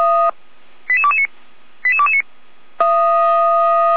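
Five-tone selective call (ZVEI), heard as quick runs of five tones, tones of about 70 ms. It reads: 88229 88229
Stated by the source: ZVEI five-tone selective call received on an SDR, fire brigade alerting.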